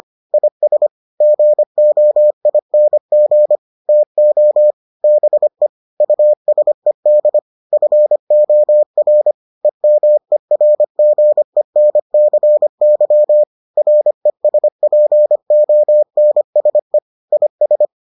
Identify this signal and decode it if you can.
Morse code, heard as a keyed tone, about 25 wpm, 600 Hz. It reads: IS GOING TO BE USED FOR EMERGENCY RESPONSE IS